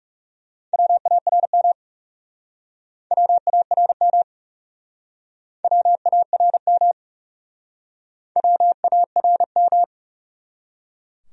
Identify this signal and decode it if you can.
Morse code, heard as a keyed tone, about 39 wpm, 700 Hz. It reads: WARM WARM WARM WARM